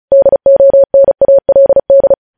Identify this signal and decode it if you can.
Morse code, heard as a keyed tone, about 36 wpm, 550 Hz. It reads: DONALD
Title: News Headlines at 35